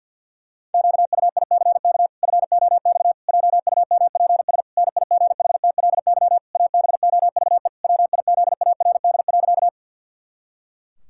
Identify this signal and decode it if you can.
Morse code, heard as a keyed tone, about 52 wpm, 700 Hz. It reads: QUICK FOX JUMPS NIGHTLY ABOVE WIZARD.
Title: Pangram Challenge v2 - 50wpm